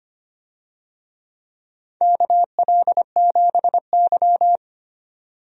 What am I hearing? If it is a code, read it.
Morse code, heard as a keyed tone, about 25 wpm, 700 Hz. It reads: KL7Y
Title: DXCC-100 Call Signs - 4 Characters - 25wpm